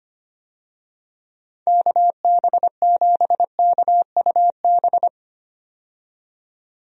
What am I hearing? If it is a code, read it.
Morse code, heard as a keyed tone, about 25 wpm, 700 Hz. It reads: KB7KUB
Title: USA Call Signs - 6 Characters - 25wpm